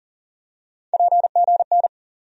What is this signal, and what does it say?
Morse code, heard as a keyed tone, about 40 wpm, 700 Hz. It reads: PGN
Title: USA Call Sign Postfixes - 3 Letters 40wpm